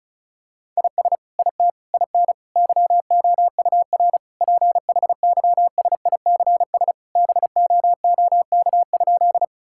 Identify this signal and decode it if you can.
Morse code, heard as a keyed tone, about 36 wpm, 700 Hz. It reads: IS IT IN YOUR PHYSICS BOOK?
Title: Sentences from Top 1000 Words - Review Part 5 - 35wpm